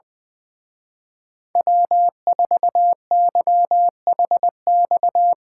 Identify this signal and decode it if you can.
Morse code, heard as a keyed tone, about 21 wpm, 700 Hz. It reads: W4YHX
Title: Speed Racing - USA Call Signs 20wpm